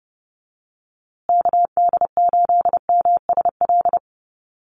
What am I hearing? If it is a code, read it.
Morse code, heard as a keyed tone, about 30 wpm, 700 Hz. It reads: KD8MSL
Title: USA Call Signs Repeated 3x 30wpm